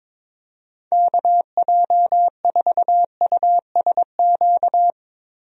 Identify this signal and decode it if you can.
Morse code, heard as a keyed tone, about 22 wpm, 700 Hz. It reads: KJ4USQ
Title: USA Call Signs - 6 Characters - Repeated 2x - 22wpm